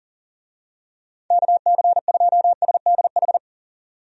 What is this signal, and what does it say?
Morse code, heard as a keyed tone, about 40 wpm, 700 Hz. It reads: KC2SDH
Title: USA Call Signs - 6 Characters - 40wpm